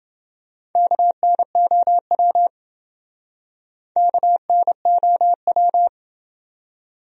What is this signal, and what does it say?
Morse code, heard as a keyed tone, about 28 wpm, 700 Hz. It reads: KNOW KNOW